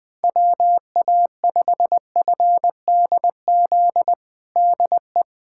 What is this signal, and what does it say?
Morse code, heard as a keyed tone, about 20 wpm, 700 Hz. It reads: WA5FDZ DE
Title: USA Callsign DE Callsign 20wpm